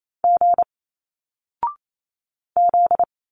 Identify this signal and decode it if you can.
Morse code, heard as a keyed tone, about 28 wpm, 700 Hz. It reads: G Z
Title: Single Letters 28wpm